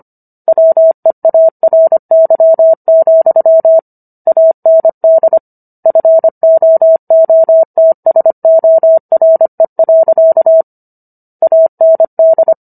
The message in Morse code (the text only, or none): WEARY, AND FOOTSORE. AND